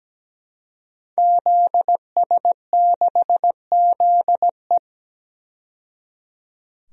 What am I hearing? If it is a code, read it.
Morse code, heard as a keyed tone, about 17 wpm, 700 Hz. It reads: ZS6ZE
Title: DXCC-100 Call Signs - 5 Characters - 17wpm